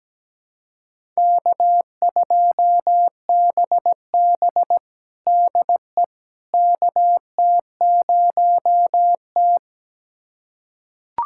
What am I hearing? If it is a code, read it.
Morse code, heard as a keyed tone, about 17 wpm, 700 Hz. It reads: K2BB DE KT0T
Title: USA Callsign DE Callsign - 4 Characters - 17wpm